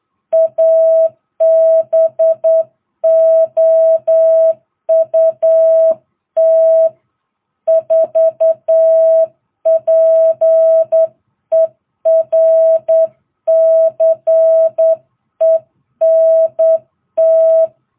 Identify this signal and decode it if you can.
Morse code, heard as a keyed tone, about 9 wpm, 660 Hz. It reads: ABOUT 4 PERCENT